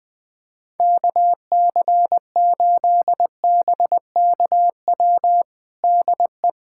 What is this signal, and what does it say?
Morse code, heard as a keyed tone, about 20 wpm, 700 Hz. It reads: KC8BKW DE
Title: USA Callsign DE Callsign - 6 Characters - 20wpm